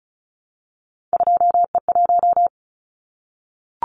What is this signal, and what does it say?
Morse code, heard as a keyed tone, about 34 wpm, 700 Hz. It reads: 2E1
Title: Callsign Prefixes for Top 25 DXCC - Repeated 3x - 35wpm